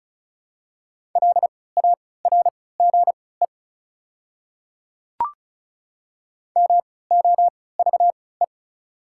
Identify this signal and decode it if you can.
Morse code, heard as a keyed tone, about 34 wpm, spaced as for 20 wpm, 690 Hz. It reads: LARGE MOVE
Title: 3x Character Spacing - Top 300 Words - 35wpm